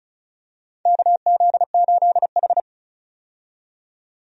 Morse code, heard as a keyed tone, about 36 wpm, 690 Hz. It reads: KZ8H
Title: USA Call Signs - 4 Characters - 35wpm